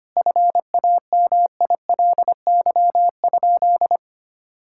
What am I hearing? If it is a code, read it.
Morse code, heard as a keyed tone, about 25 wpm, 700 Hz. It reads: FAMILY?
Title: Sentences from Top 1000 Words - Review Part 5 - 25wpm